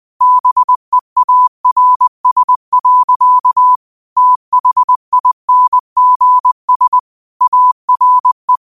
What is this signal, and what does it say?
Morse code, heard as a keyed tone, about 20 wpm, 1000 Hz. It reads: BEARS. THINGS ARE